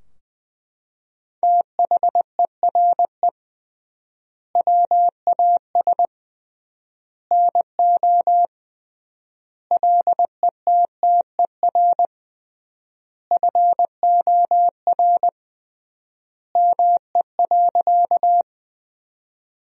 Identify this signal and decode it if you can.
Morse code, heard as a keyed tone, about 20 wpm, 700 Hz. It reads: THERE WAS NO LETTER FOR ME.
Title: 3x Word-Spacing - Sentences from Top 300 Words - 20wpm